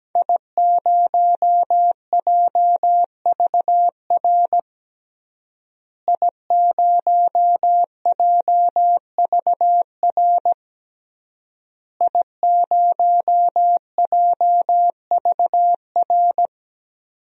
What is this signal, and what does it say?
Morse code, heard as a keyed tone, about 17 wpm, 700 Hz. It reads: I0JVR I0JVR I0JVR
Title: DXCC-100 Call Signs - 5 Characters - Repeated 3x - 17wpm